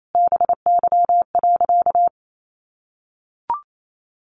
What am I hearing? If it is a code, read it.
Morse code, heard as a keyed tone, about 28 wpm, 700 Hz. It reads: BY.
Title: Sentences from Top 1500 Words 28wpm